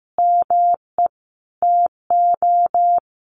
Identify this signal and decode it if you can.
Morse code, heard as a keyed tone, about 14 wpm, 700 Hz. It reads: ME TO
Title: Sentences from Top 1200 Words 15wpm